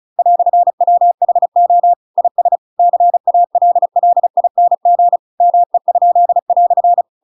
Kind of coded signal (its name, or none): Morse code